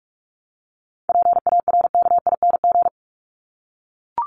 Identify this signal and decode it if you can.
Morse code, heard as a keyed tone, about 44 wpm, 700 Hz. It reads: PARKING
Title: Top 1501-1750 Words 45wpm